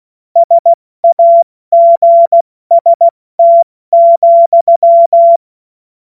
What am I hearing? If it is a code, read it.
Morse code, heard as a keyed tone, about 16 wpm, 680 Hz. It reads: SAGST,
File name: Morsekurs / Witze / Witz 03. August 2021, 70 F80